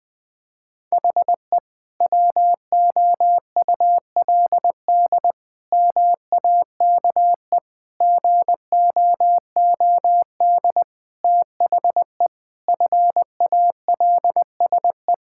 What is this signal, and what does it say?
Morse code, heard as a keyed tone, about 20 wpm, 700 Hz. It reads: HE WOULD MAKE GOOD THE FALSE
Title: Sentences from Top 1600 Words 20wpm